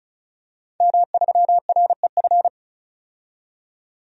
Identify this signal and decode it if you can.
Morse code, heard as a keyed tone, about 36 wpm, 700 Hz. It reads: M3REF
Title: DXCC-100 Call Signs - 5 Characters - 35wpm